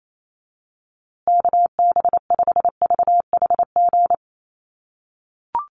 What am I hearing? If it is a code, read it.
Morse code, heard as a keyed tone, about 28 wpm, 700 Hz. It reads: KB5VHG
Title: USA Call Signs Repeated 3x 28wpm